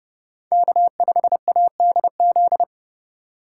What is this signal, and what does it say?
Morse code, heard as a keyed tone, about 30 wpm, 700 Hz. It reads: K5ADZ
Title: USA Call Signs - 5 Characters - 4x Character Spacing - 30wpm